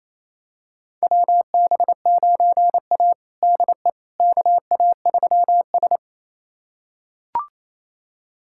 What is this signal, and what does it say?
Morse code, heard as a keyed tone, about 27 wpm, 700 Hz. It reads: WB9A DE KA3S
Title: USA Callsign DE Callsign - 4 Characters - 28wpm